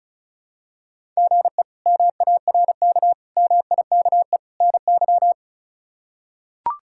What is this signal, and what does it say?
Morse code, heard as a keyed tone, about 36 wpm, 700 Hz. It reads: GE MARK MIKE NY
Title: K1USN SST QSO Practice - 35wpm